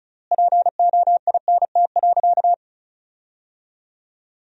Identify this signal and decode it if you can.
Morse code, heard as a keyed tone, about 34 wpm, 700 Hz. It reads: POINT.
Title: 4x Word-Spacing - Sentences from Top 200 Words - 35wpm